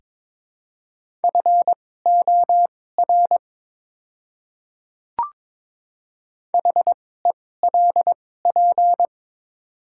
Morse code, heard as a keyed tone, about 22 wpm, spaced as for 16 wpm, 700 Hz. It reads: FOR HELP
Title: 2x Character Spacing - Top 200 Words - 22wpm